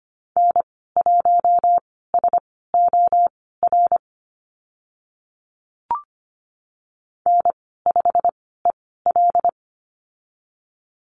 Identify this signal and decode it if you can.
Morse code, heard as a keyed tone, about 25 wpm, spaced as for 16 wpm, 690 Hz. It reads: N1SOR N5EL
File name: Lesson 15g - Number 5 - Call Signs 25wpm